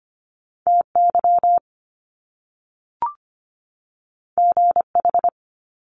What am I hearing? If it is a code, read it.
Morse code, heard as a keyed tone, about 25 wpm, 700 Hz. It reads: TY GH
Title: Top Two Letter Combinations in Words 25wpm